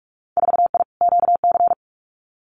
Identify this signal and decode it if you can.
Morse code, heard as a keyed tone, about 46 wpm, 700 Hz. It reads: 4I QC
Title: Winter Field Day Exchanges Encore 45wpm